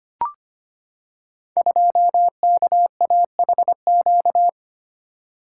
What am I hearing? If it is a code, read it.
Morse code, heard as a keyed tone, about 25 wpm, 700 Hz. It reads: 2KAHQ
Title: Code Groups - 5 Letter:Numbers - 25wpm